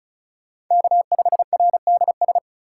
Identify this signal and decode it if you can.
Morse code, heard as a keyed tone, about 45 wpm, 700 Hz. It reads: K5RDS